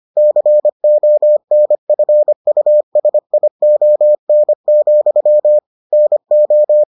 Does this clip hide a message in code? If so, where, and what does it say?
Yes: Morse code from 0.2 to 6.8 s, heard as a keyed tone, about 25 wpm, 600 Hz. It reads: CONFUSION, NO